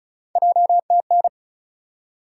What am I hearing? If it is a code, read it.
Morse code, heard as a keyed tone, about 36 wpm, 700 Hz. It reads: JTN
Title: USA Call Sign Postfixes - 3 Letters 35wpm